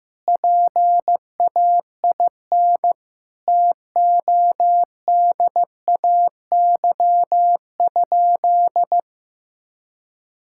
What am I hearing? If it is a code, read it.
Morse code, heard as a keyed tone, about 16 wpm, 700 Hz. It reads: PAIN TODAY?
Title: Sentences from Top 1600 Words 15wpm